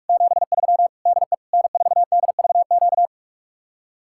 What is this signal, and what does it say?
Morse code, heard as a keyed tone, about 50 wpm, 700 Hz. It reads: 73 DE N4DVQ